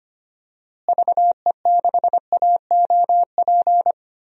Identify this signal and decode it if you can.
Morse code, heard as a keyed tone, about 25 wpm, 700 Hz. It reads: VE6AOP
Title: DXCC-100 Call Signs - 6 Characters - Repeated 2x - 25wpm